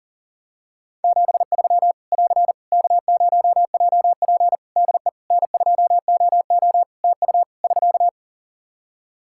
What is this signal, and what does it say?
Morse code, heard as a keyed tone, about 37 wpm, 700 Hz. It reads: 73 <AR> K0JP DE N2OO TU <SK>
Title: Practice QSO - Exchange 4 - Ending - 40wpm